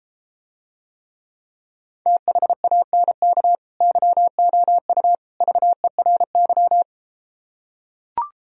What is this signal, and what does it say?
Morse code, heard as a keyed tone, about 33 wpm, 700 Hz. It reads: THANK YOU VERY